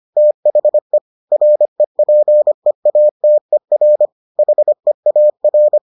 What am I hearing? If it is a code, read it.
Morse code, heard as a keyed tone, about 25 wpm, 600 Hz. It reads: THE REPEATER HEAR